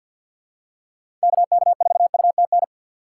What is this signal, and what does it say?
Morse code, heard as a keyed tone, about 53 wpm, 700 Hz. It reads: KK4UTN
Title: USA Call Signs - 6 Characters - 4x Character Spacing - 50wpm